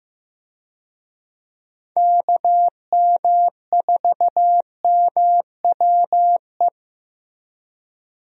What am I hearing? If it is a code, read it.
Morse code, heard as a keyed tone, about 16 wpm, 700 Hz. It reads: KM4MWE
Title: USA Call Signs - 6 Characters - Repeated 3x - 15wpm